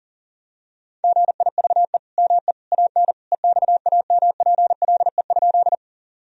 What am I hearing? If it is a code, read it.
Morse code, heard as a keyed tone, about 40 wpm, 700 Hz. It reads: GIVE ME AN EXAMPLE?
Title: Sentences from Top 1000 Words - Review Part 1 - 40wpm